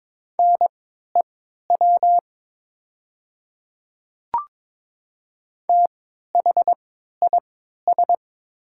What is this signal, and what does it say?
Morse code, heard as a keyed tone, about 22 wpm, spaced as for 12 wpm, 700 Hz. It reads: NEW THIS
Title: 3x Character Spacing - Top 200 Words - 22wpm